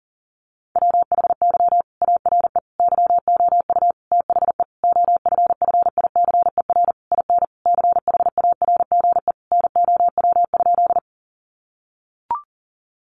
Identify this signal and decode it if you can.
Morse code, heard as a keyed tone, about 37 wpm, 700 Hz. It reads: WHY ARE YOU THE OFFICER IN CHARGE NOW?
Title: Sentences from Top 1200 Words 40wpm